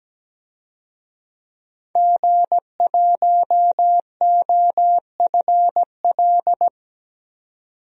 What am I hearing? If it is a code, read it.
Morse code, heard as a keyed tone, about 17 wpm, 700 Hz. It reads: G1OFL
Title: DXCC-100 Call Signs - 5 Characters - 17wpm